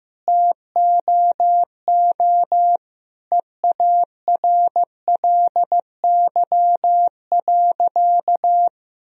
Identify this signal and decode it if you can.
Morse code, heard as a keyed tone, about 15 wpm, 700 Hz. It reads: TOO EARLY.
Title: Sentences from Top 1000 Words - Review Part 2 - 15wpm